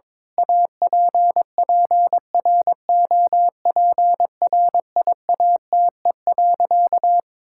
Morse code, heard as a keyed tone, about 22 wpm, 700 Hz. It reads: APPROPRIATE.
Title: Sentences from Top 1000 Words - Review Part 5 - 22wpm